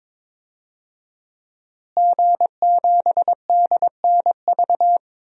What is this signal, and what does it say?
Morse code, heard as a keyed tone, about 22 wpm, 700 Hz. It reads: G7DNV
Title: DXCC-100 Call Signs - 5 Characters - Repeated 3x - 22wpm